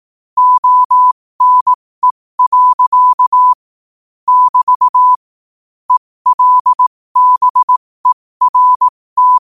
Morse code, heard as a keyed tone, about 18 wpm, spaced as for 15 wpm, 1000 Hz. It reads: ONE. <BT> ELBERT
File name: qotdfiles / 15wpm